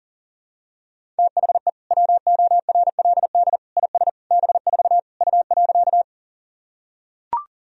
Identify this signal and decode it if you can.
Morse code, heard as a keyed tone, about 40 wpm, 700 Hz. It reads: THE WORLD IS B4 U.
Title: Sentences from Top 200 Words and Abbreviations 40wpm